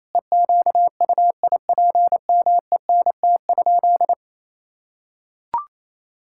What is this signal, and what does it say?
Morse code, heard as a keyed tone, about 28 wpm, 700 Hz. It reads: EQUIPMENT?
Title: Sentences from Top 1000 Words - Review Part 5 - 28wpm